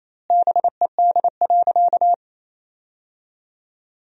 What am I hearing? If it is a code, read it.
Morse code, heard as a keyed tone, about 28 wpm, 700 Hz. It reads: BED.
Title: 2x Word-Spacing - Sentences from Top 500 Words - 28wpm